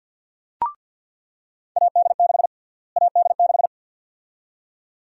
Morse code, heard as a keyed tone, about 48 wpm, 700 Hz. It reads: AD6 AD6
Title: USA Call Sign Prefixes - Repeated 2x - 50wpm